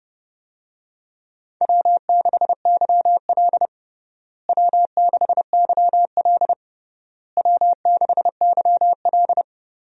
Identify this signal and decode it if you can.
Morse code, heard as a keyed tone, about 30 wpm, 700 Hz. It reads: W6YL W6YL W6YL
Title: USA Call Signs - 4 Characters - Repeated 3x - 30wpm